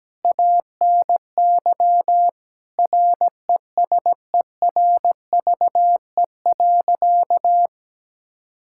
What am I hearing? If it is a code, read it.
Morse code, heard as a keyed tone, about 17 wpm, 700 Hz. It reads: ANY RESERVE.
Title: Sentences from Top 1000 Words - Review Part 6 - 17wpm